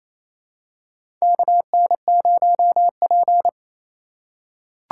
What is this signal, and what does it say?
Morse code, heard as a keyed tone, about 28 wpm, 700 Hz. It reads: KN0P
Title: USA Call Signs - 4 Characters - 4x Character Spacing - 28wpm